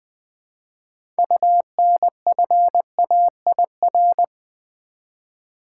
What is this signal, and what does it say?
Morse code, heard as a keyed tone, about 20 wpm, 700 Hz. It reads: UNFAIR
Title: Six Letter Words - 20wpm